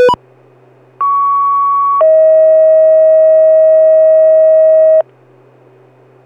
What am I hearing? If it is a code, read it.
Two-tone page, heard as a one-second tone, then a three-second tone, about 1120 then 640 Hz. It is heard from 1.0 to 5.0 s.